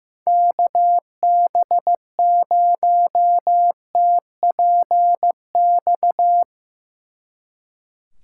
Morse code, heard as a keyed tone, about 16 wpm, 700 Hz. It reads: KB0TPX
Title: Speed Racing - USA 2x3 Call Signs 15wpm